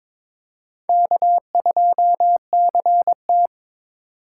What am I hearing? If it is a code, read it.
Morse code, heard as a keyed tone, about 22 wpm, 700 Hz. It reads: K2CT
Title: USA Call Signs - 4x Character Spacing - 22wpm